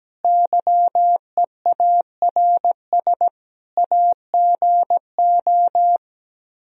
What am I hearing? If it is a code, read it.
Morse code, heard as a keyed tone, about 17 wpm, 700 Hz. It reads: YEARS AGO
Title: Speed-Racing - Sets of 5 Words 17wpm